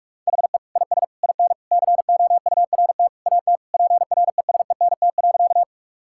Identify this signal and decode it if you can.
Morse code, heard as a keyed tone, about 45 wpm, 700 Hz. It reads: HE IS IN COURT AT PRESENT.
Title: Sentences from Top 1600 Words 45wpm